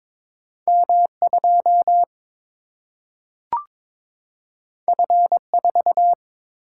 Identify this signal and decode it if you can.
Morse code, heard as a keyed tone, about 22 wpm, 700 Hz. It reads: M2 F4
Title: Single-Letter Single-Number 22wpm